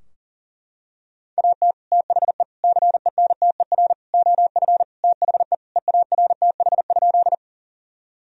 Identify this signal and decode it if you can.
Morse code, heard as a keyed tone, about 40 wpm, 700 Hz. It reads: AT THE CENTER OF THE EARTH?
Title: Sentences from Top 1000 Words - Review Part 2 - 40wpm